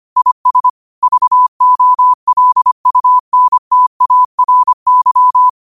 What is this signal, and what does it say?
Morse code, heard as a keyed tone, about 25 wpm, 1000 Hz. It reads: IS VOLUNTARY
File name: qotdfiles / 25wpm